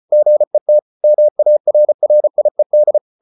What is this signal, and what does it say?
Morse code, heard as a keyed tone, about 34 wpm, 600 Hz. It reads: GET MARRIED